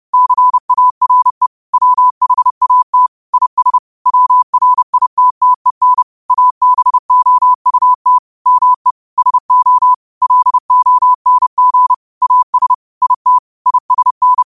Morse code, heard as a keyed tone, about 30 wpm, 1000 Hz. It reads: CARE WHAT IS WRITTEN ABOUT ME SO LONG AS IT ISN